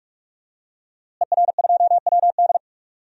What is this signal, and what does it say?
Morse code, heard as a keyed tone, about 44 wpm, 700 Hz. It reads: ER2WD
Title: DXCC-100 Call Signs - 5 Characters - Repeated 3x - 45wpm